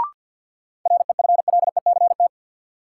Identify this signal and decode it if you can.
Morse code, heard as a keyed tone, about 51 wpm, 700 Hz. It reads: REFLECT
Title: Sentences from Top 1500 Words - New Word Drill - 50wpm